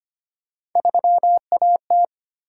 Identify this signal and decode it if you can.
Morse code, heard as a keyed tone, about 25 wpm, 700 Hz. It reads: 3AT
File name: Three Letter-Number Combinations - 25wpm